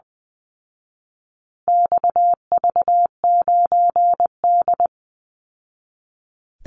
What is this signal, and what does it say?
Morse code, heard as a keyed tone, about 20 wpm, 700 Hz. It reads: XV9D
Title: Call Signs Repeated 3x - 20wpm